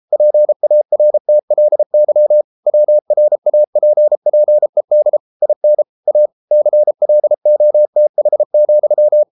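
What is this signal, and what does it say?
Morse code, heard as a keyed tone, about 33 wpm, 600 Hz. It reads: PARTLY WRAPPED IN A CLOTH,
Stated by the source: Morse code practice at 32 wpm from CW med Gnister, reading Ken Follett's novel Never in English.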